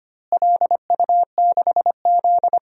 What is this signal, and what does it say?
Morse code, heard as a keyed tone, about 25 wpm, 700 Hz. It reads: LU6Z
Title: DXCC-100 Call Signs - 4 Characters - Repeated 2x - 25wpm